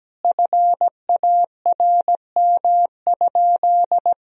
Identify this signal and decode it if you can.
Morse code, heard as a keyed tone, about 17 wpm, 700 Hz. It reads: FARM?
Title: Sentences from Top 1000 Words - Review Part 3 - 17wpm